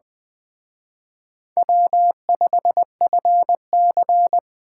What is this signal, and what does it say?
Morse code, heard as a keyed tone, about 20 wpm, 700 Hz. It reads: W5FC
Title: Speed Racing - USA 1x2 Call Signs 17wpm